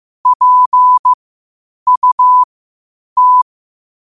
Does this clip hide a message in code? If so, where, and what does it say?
Yes: Morse code from 0.3 to 3.4 s, heard as a keyed tone, about 15 wpm, spaced as for 8 wpm, 1000 Hz. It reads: PUT